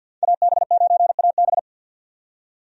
Morse code, heard as a keyed tone, about 50 wpm, 700 Hz. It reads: AB9AB
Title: USA Call Signs - 5 Characters - 50wpm